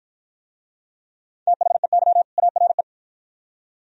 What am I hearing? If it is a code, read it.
Morse code, heard as a keyed tone, about 53 wpm, 700 Hz. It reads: THEY ARE